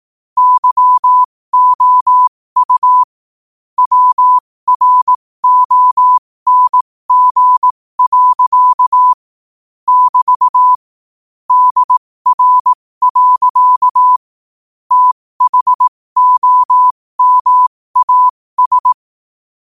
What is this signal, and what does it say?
Morse code, heard as a keyed tone, about 18 wpm, spaced as for 15 wpm, 1000 Hz. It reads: YOU WRONG. <BT> DR. THOMAS
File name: qotdfiles / 15wpm